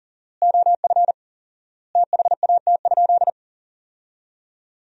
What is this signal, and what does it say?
Morse code, heard as a keyed tone, about 40 wpm, 700 Hz. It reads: OF THAT?
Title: 4x Word-Spacing - Sentences from Top 100 Words - 40wpm